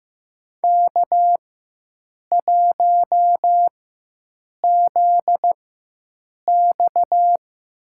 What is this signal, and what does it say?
Morse code, heard as a keyed tone, about 15 wpm, spaced as for 7 wpm, 700 Hz. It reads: K1ZX
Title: USA Call Signs - 4x Character Spacing - 15wpm